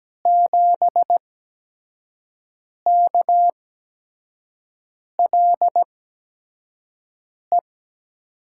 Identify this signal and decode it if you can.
Morse code, heard as a keyed tone, about 17 wpm, spaced as for 5 wpm, 700 Hz. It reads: 7KLE
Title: USA Call Signs - 6 Characters - 8x Character Spacing - 17wpm